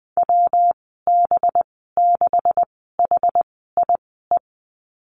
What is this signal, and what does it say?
Morse code, heard as a keyed tone, about 20 wpm, 700 Hz. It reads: W B 6 H I E
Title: 207 Lesson 33g - Letter j - Call Signs 20wpm